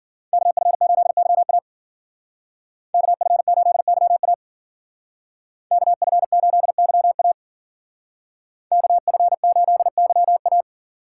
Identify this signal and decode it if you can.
Morse code, heard as a keyed tone, about 52 wpm, 700 Hz. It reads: KF8YA KF8YA KF8YA KF8YA